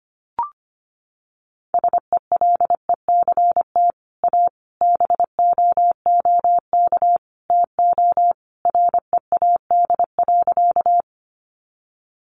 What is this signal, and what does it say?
Morse code, heard as a keyed tone, about 25 wpm, 700 Hz. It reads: SELECT A BOOK TO READ.
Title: Sentences from Top 1300 Words 25wpm